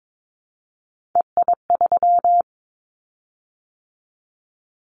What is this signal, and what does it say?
Morse code, heard as a keyed tone, about 22 wpm, 700 Hz. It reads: EI3
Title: Callsign Prefixes for Top 50 DXCC - 22wpm